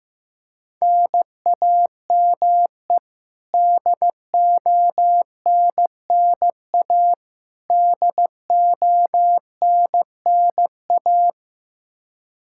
Morse code, heard as a keyed tone, about 15 wpm, 700 Hz. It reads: NAME DONNA DONNA
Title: K1USN SST QSO Practice 15wpm